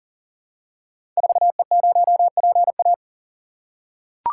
Morse code, heard as a keyed tone, about 39 wpm, 700 Hz. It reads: 4E0PA
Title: Code Groups - 5 Letter:Numbers - 40wpm